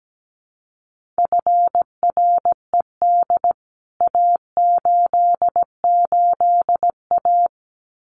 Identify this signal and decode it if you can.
Morse code, heard as a keyed tone, about 17 wpm, 700 Hz. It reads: FRED A88A
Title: CWT Exchange 17wpm